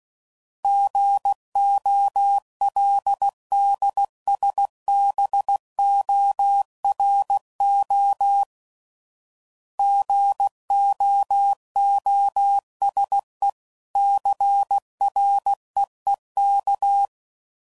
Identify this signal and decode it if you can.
Morse code, heard as a keyed tone, about 16 wpm, 800 Hz. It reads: GOLDSBORO GOOSE CREEK
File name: CW Code Groups @15WPM / fromSM's / Cities